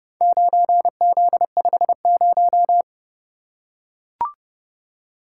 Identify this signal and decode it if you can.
Morse code, heard as a keyed tone, about 30 wpm, 700 Hz. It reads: 9Z50